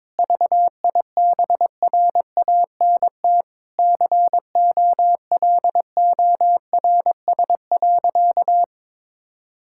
Morse code, heard as a keyed tone, about 22 wpm, 700 Hz. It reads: VIBRANT COLORS.
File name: Difficult Sentences with Hints - 22wpm